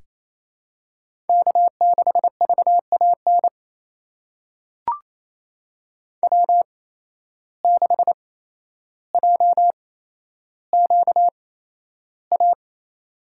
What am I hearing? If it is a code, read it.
Morse code, heard as a keyed tone, about 28 wpm, spaced as for 12 wpm, 700 Hz. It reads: K6VAN W6JQA
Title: USA Call Signs - 5 Characters - 8x Character Spacing - 28wpm